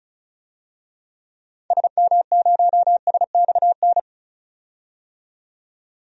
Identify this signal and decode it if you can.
Morse code, heard as a keyed tone, about 34 wpm, 700 Hz. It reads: SM0SXN
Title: DXCC-100 Call Signs - 6 Characters - 35wpm